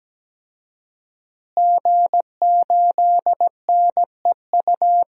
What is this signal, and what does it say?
Morse code, heard as a keyed tone, about 17 wpm, 700 Hz. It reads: G8NEU
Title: DXCC-100 Call Signs - 5 Characters - Repeated 2x - 17wpm